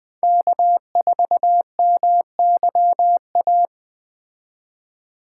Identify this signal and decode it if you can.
Morse code, heard as a keyed tone, about 20 wpm, 700 Hz. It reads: K4MYA
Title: Speed Racing - USA 1x3 Call Signs 17wpm